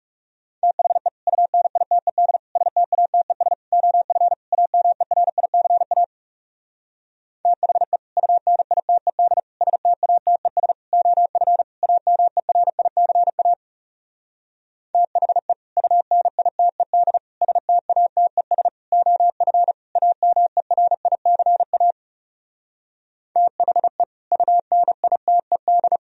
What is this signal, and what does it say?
Morse code, heard as a keyed tone, about 40 wpm, 700 Hz. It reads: THE UNITED STATES OF AMERICA THE UNITED STATES OF AMERICA THE UNITED STATES OF AMERICA THE UNITED